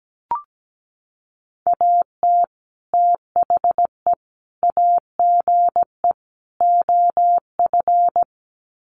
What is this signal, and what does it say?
Morse code, heard as a keyed tone, about 17 wpm, 700 Hz. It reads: AT THE AGE OF
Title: Sets of 4 Words 17wpm